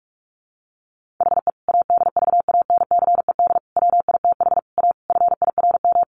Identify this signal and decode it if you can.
Morse code, heard as a keyed tone, about 44 wpm, 700 Hz. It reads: HE ADVANCED WITH A FIRM
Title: Sentences from Top 1400 Words 45wpm